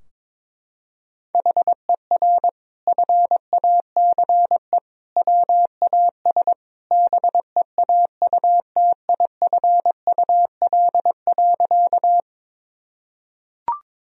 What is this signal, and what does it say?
Morse code, heard as a keyed tone, about 22 wpm, 700 Hz. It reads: HER FACE WAS BEAUTIFUL.
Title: Sentences from Top 1000 Words - Review Part 2 - 22wpm